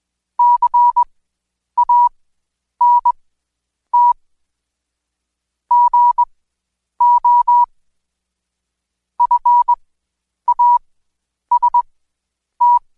Morse code, heard as a keyed tone, about 20 wpm, spaced as for 9 wpm, 970 Hz. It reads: CANT GO FAST
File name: Code Quick / AudioDownld2 / Seg115-Day20-DayDrill 20